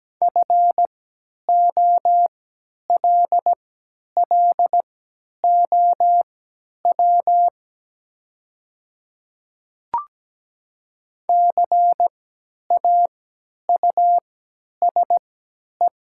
Morse code, heard as a keyed tone, about 17 wpm, spaced as for 10 wpm, 700 Hz. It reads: FOLLOW CAUSE
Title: 3x Character Spacing - Top 300 Words - 17wpm